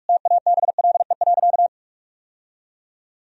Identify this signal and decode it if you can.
Morse code, heard as a keyed tone, about 46 wpm, 700 Hz. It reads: TABLE.
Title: Sentences from Top 1000 Words - Review Part 2 - 45wpm